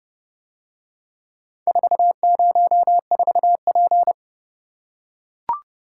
Morse code, heard as a keyed tone, about 30 wpm, 700 Hz. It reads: 404P